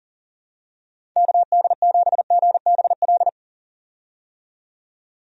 Morse code, heard as a keyed tone, about 37 wpm, 700 Hz. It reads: KD7GBL
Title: USA Call Signs - 6 Characters - 40wpm